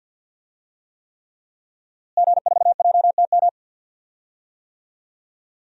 Morse code, heard as a keyed tone, about 51 wpm, 700 Hz. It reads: G4JTM
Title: DXCC-100 Call Signs - 5 Characters - Repeated 3x - 50wpm